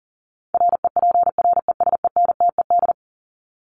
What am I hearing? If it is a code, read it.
Morse code, heard as a keyed tone, about 40 wpm, 700 Hz. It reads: REPRESENTED
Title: ED Suffix - Top 100 Words with Base Words 40wpm